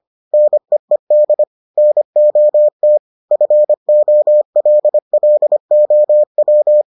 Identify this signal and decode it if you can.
Morse code, heard as a keyed tone, about 25 wpm, 600 Hz. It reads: NEED NOT FOLLOW